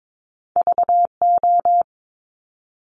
Morse code, heard as a keyed tone, about 22 wpm, 700 Hz. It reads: VO
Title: Top Two Letter Combinations in Words 22wpm